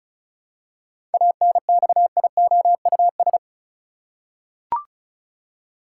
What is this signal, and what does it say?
Morse code, heard as a keyed tone, about 34 wpm, 700 Hz. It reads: ANXIOUS
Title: Seven Letter Words - 35wpm